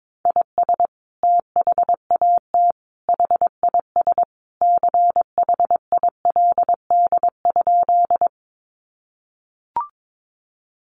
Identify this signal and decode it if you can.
Morse code, heard as a keyed tone, about 22 wpm, 700 Hz. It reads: IS THAT HIS CHILD?
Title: Sentences from Top 800 Words 22wpm